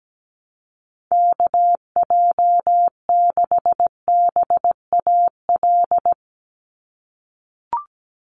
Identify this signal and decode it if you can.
Morse code, heard as a keyed tone, about 17 wpm, 700 Hz. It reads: KJ6BAL
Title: USA Call Signs 17wpm